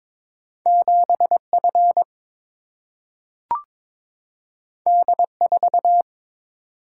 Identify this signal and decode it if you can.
Morse code, heard as a keyed tone, about 22 wpm, 700 Hz. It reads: 7 F D 4
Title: Single-Letter Single-Number 22wpm